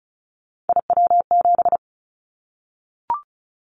Morse code, heard as a keyed tone, about 34 wpm, 700 Hz. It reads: IW7
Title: Callsign Prefixes for Top 25 DXCC - 35wpm